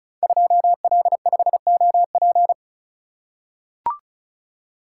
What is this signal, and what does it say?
Morse code, heard as a keyed tone, about 36 wpm, 700 Hz. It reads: 2L5OP
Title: Code Groups - 5 Letter:Numbers - 35wpm